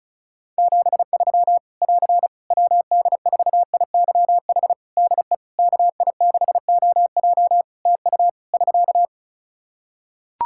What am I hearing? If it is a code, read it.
Morse code, heard as a keyed tone, about 34 wpm, 700 Hz. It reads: 73 <AR> WD4IYH DE KI6OJ TU <SK>
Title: K1USN SST QSO Practice 35wpm